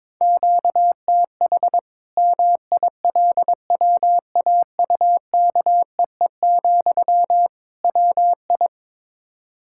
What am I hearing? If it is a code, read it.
Morse code, heard as a keyed tone, about 21 wpm, 700 Hz. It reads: QTH MILWAUKEE, WI
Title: Practice QSO - Exchange 1 - Beginning - 22wpm